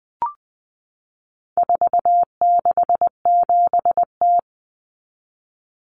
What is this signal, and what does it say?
Morse code, heard as a keyed tone, about 20 wpm, 700 Hz. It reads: 467T
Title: Common Cut Numbers - 4-Digit 20wpm